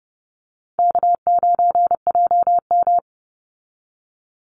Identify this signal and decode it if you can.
Morse code, heard as a keyed tone, about 30 wpm, 700 Hz. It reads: K9JM
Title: Easy Call Signs Repeated 3x - 30wpm